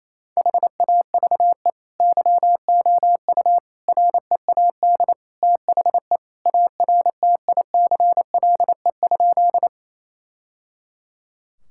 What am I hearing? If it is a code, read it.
Morse code, heard as a keyed tone, about 28 wpm, 700 Hz. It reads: HAVE YOU READ THE ARTICLE?
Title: Sentences from Top 1000 Words - Review Part 5 - 28wpm